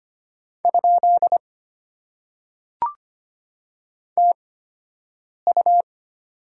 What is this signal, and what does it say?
Morse code, heard as a keyed tone, about 25 wpm, spaced as for 7 wpm, 700 Hz. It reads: ? TU
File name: Top 90 QSO Elements - 8x Character Spacing - 25wpm